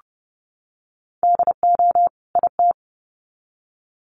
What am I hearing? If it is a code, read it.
Morse code, heard as a keyed tone, about 30 wpm, 700 Hz. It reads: DO IT
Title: Sets of 2 Words 30wpm